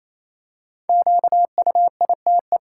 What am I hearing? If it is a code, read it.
Morse code, heard as a keyed tone, about 28 wpm, 700 Hz. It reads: QUITE
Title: Five Letter Words - 28wpm